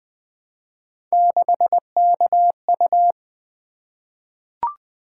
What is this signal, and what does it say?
Morse code, heard as a keyed tone, about 20 wpm, 700 Hz. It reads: 6KU
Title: Three Letter-Number Combinations - 20wpm